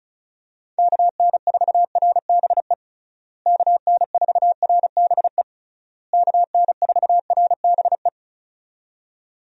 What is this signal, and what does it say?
Morse code, heard as a keyed tone, about 34 wpm, 700 Hz. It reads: KN4RBE KN4RBE KN4RBE
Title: USA Call Signs - 6 Characters - Repeated 3x - 35wpm